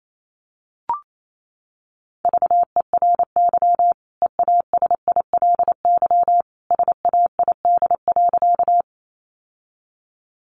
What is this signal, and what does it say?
Morse code, heard as a keyed tone, about 28 wpm, 700 Hz. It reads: VERY EASILY SAID.
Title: Sentences from Top 900 Words 28wpm